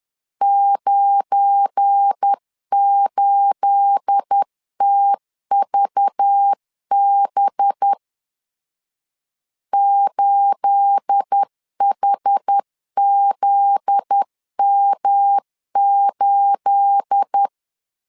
Morse code, heard as a keyed tone, about 10 wpm, 800 Hz. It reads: 98TVB 8HZM8
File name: LEZIONI / 9 28 10wpm 2842